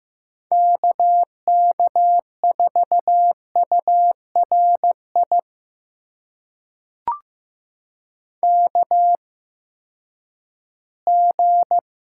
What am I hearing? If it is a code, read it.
Morse code, heard as a keyed tone, about 15 wpm, spaced as for 10 wpm, 700 Hz. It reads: KK4URI KG
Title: USA Call Signs - 6 Characters - 8x Character Spacing - 15wpm